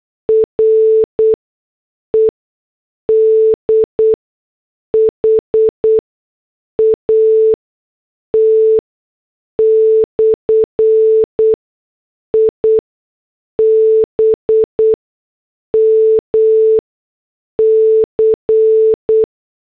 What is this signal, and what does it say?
Morse code, heard as a keyed tone, about 8 wpm, spaced as for 6 wpm, 440 Hz. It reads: REDHAT/IBM C